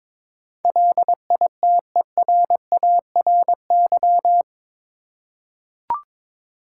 Morse code, heard as a keyed tone, about 22 wpm, 700 Hz. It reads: LITERARY
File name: practice / 22wpm / Eight Letter Words - 22wpm